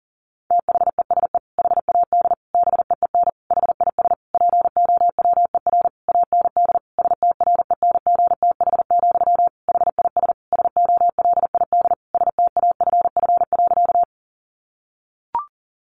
Morse code, heard as a keyed tone, about 40 wpm, 700 Hz. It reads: THESE HAD BEEN HIS POWER AND STRENGTH, HIS SOLID STAFF.
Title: Sentences from Top 1500 Words 40wpm